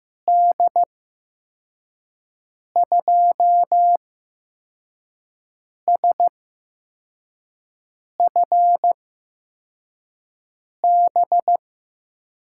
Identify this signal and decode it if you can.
Morse code, heard as a keyed tone, about 15 wpm, spaced as for 4 wpm, 700 Hz. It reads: D2SFB
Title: USA Call Signs - 6 Characters - 8x Character Spacing - 15wpm